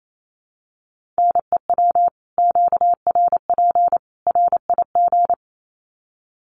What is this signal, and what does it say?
Morse code, heard as a keyed tone, about 28 wpm, 700 Hz. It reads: NEW QRP RIG
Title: Common QSO Phrases 28wpm